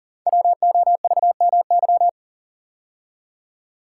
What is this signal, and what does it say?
Morse code, heard as a keyed tone, about 40 wpm, 700 Hz. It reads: WOVMY
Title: Code Groups - 5 Letters - 40wpm